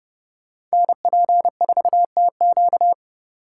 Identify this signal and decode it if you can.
Morse code, heard as a keyed tone, about 36 wpm, 700 Hz. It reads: NP4TQ
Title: Speed Racing - USA 2x2 Call Signs 30wpm